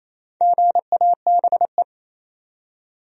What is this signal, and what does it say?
Morse code, heard as a keyed tone, about 28 wpm, 700 Hz. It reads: GABE
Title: USA Common Names - 28wpm